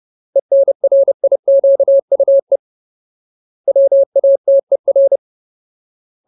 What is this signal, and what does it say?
Morse code, heard as a keyed tone, about 30 wpm, 550 Hz. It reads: ENRIQUE WATER
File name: Ultimate Word List / 30WPM Ultimate